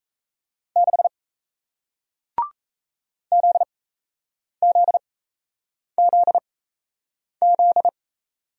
Morse code, heard as a keyed tone, about 37 wpm, spaced as for 8 wpm, 700 Hz. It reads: 6 ZZZZ